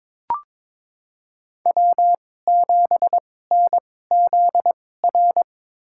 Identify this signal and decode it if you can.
Morse code, heard as a keyed tone, about 22 wpm, spaced as for 16 wpm, 700 Hz. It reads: W7NZR
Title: USA Call Signs - 5 Characters - 2x Character Spacing - 22wpm